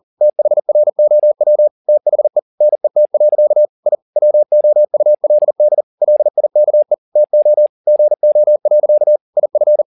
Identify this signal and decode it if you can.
Morse code, heard as a keyed tone, about 40 wpm, 600 Hz. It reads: THROW THE NET. I WOULD LIKE TO GO. IF